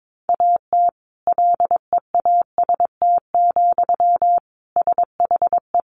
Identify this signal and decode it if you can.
Morse code, heard as a keyed tone, about 22 wpm, 700 Hz. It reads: AT LEAST, SHE
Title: Sentences from Top 1400 Words 22wpm